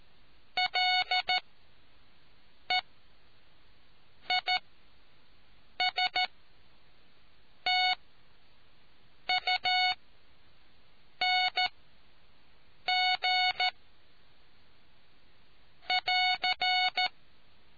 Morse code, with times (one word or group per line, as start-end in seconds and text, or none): 0.6-13.7 LEISTUNG
15.9-17.1 <AR>